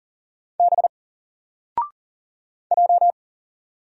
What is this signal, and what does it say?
Morse code, heard as a keyed tone, about 40 wpm, 700 Hz. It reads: B J
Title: Single Letter-Number 40wpm